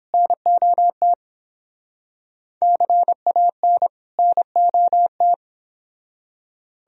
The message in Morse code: NOT CAN NOT

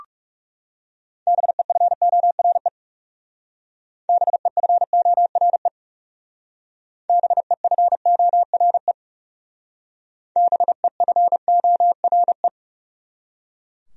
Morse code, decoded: BEFORE BEFORE BEFORE BEFORE

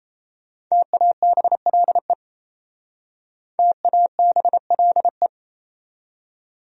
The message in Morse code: TABLE TABLE

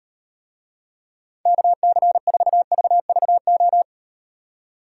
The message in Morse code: KC4VVO

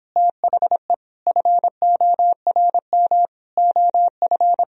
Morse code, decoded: THE FORM OF

